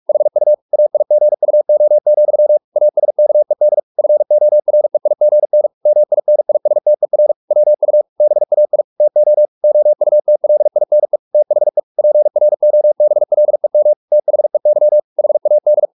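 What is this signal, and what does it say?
Morse code, heard as a keyed tone, about 44 wpm, 600 Hz. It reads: HU AIGUO, ASKED FOREIGN MINISTER WU BAI TO OUTLINE THE PROBLEM THEY HAD
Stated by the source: Morse practice at 45 wpm, the novel Never by Ken Follett in English, from CW med Gnister.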